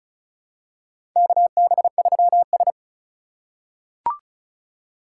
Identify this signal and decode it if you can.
Morse code, heard as a keyed tone, about 36 wpm, 700 Hz. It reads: KB3S
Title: USA Call Signs - 4 Characters - 2x Character Spacing - 35wpm